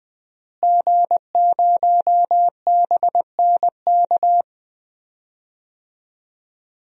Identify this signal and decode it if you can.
Morse code, heard as a keyed tone, about 20 wpm, 700 Hz. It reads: G0BNK
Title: DXCC-100 Call Signs - 5 Characters - Repeated 2x - 20wpm